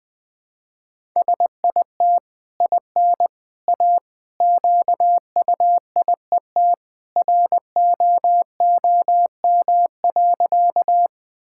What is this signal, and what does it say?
Morse code, heard as a keyed tone, about 20 wpm, 700 Hz. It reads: SIT IN A QUIET ROOM.
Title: Easy Sentences with Hints - 20wpm